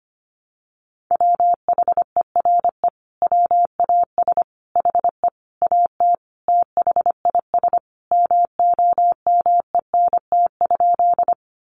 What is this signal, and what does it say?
Morse code, heard as a keyed tone, about 25 wpm, 700 Hz. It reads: WHERE WAS HE AT THIS MOMENT?
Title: Sentences from Top 1000 Words 25wpm